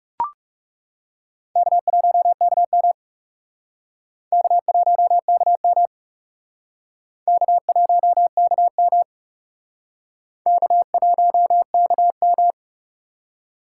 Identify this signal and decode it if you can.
Morse code, heard as a keyed tone, about 37 wpm, 700 Hz. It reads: K1KM K1KM K1KM K1KM
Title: Speed Racing - USA 1x2 Call Signs 30wpm